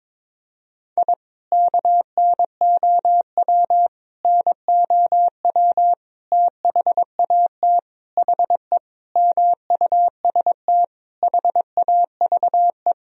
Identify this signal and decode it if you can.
Morse code, heard as a keyed tone, about 22 wpm, 700 Hz. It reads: I KNOW NOW THAT HE MUST HAVE
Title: Sentences from Top 1000 Words - Review Part 1 - 22wpm